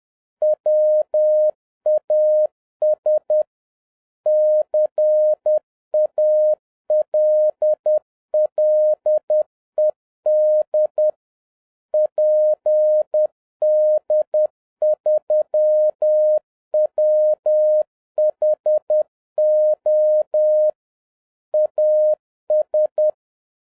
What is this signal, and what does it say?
Morse code, heard as a keyed tone, about 10 wpm, 610 Hz. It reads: WAS CALLED PD3WHO AS